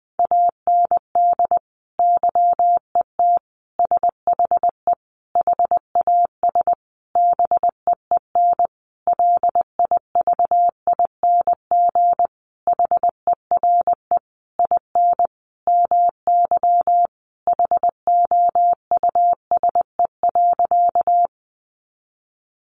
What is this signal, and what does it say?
Morse code, heard as a keyed tone, about 20 wpm, 700 Hz. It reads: AND YET SHE HAS BEEN LIVING HERE IN MY HOUSE.
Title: Sentences from Top 1100 Words 20wpm